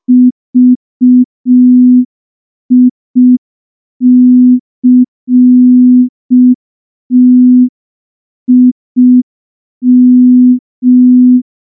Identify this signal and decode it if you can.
Morse code, heard as a keyed tone, about 5 wpm, 250 Hz. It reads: VICTIM
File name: morse / morse-training-05WPM